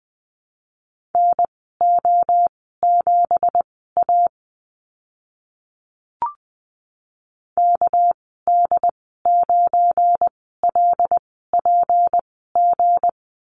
Morse code, heard as a keyed tone, about 20 wpm, spaced as for 14 wpm, 700 Hz. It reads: NO7A KD9LPG
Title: 213 Lesson 34g - Number 8 - Call Signs 20wpm